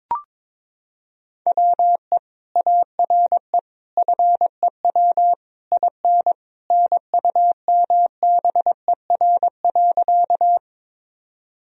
Morse code, heard as a keyed tone, about 22 wpm, 700 Hz. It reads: WE ARE FEW IN NUMBER.
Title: Sentences from Top 1000 Words - Review Part 1 - 22wpm